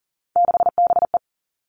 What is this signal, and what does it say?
Morse code, heard as a keyed tone, about 40 wpm, 700 Hz. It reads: 6BE
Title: All Cut Numbers - 3-Digit 40wpm